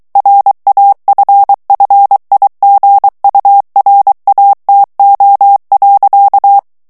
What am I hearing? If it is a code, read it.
Morse code, heard as a keyed tone, about 23 wpm, 800 Hz. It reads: RAFFIGURATO.
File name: morse mp3 / ITALIEN texte clair 1 vitesse 22 mots minute